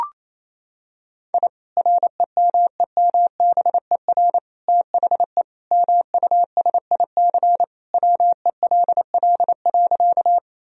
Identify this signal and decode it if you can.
Morse code, heard as a keyed tone, about 28 wpm, 700 Hz. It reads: I REMEMBER THE MUSIC WELL.
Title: Sentences from Top 1000 Words - Review Part 1 - 28wpm